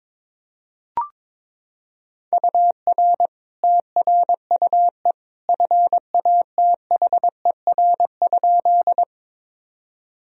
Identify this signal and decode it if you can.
Morse code, heard as a keyed tone, about 22 wpm, 700 Hz. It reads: UR TRUE FATHER?
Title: Sentences from Top 400 Words and Abbreviations 22wpm